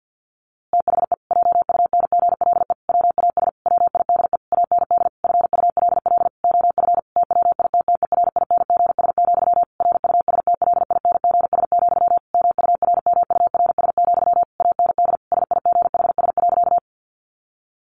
Morse code, heard as a keyed tone, about 50 wpm, 700 Hz. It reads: THE JUNGLE WAS WIDE AND FULL OF TWITTERINGS, RUSTLINGS, MURMURS, AND SIGHS.
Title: A Sound Of Thunder 50wpm